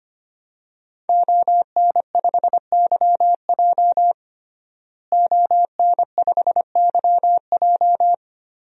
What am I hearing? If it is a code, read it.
Morse code, heard as a keyed tone, about 25 wpm, 700 Hz. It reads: ON5YJ ON5YJ
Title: DXCC-100 Call Signs - 5 Characters - Repeated 3x - 25wpm